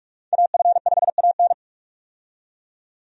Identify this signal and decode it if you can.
Morse code, heard as a keyed tone, about 43 wpm, 700 Hz. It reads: AF5AN
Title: USA Call Signs - 5 Characters - 45wpm